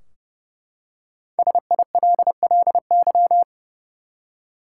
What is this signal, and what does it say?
Morse code, heard as a keyed tone, about 30 wpm, 700 Hz. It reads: SILLY
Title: Five Letter Words - 30wpm